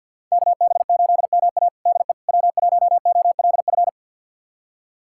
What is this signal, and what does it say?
Morse code, heard as a keyed tone, about 51 wpm, 690 Hz. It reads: KB8MA DE W1OLF
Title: USA Callsign DE Callsign - 5 Characters - 50wpm